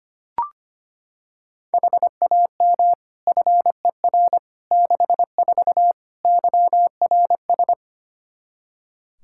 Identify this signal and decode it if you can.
Morse code, heard as a keyed tone, about 25 wpm, 700 Hz. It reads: HAM FER 64 YRS
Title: K1USN SST QSO Practice 25wpm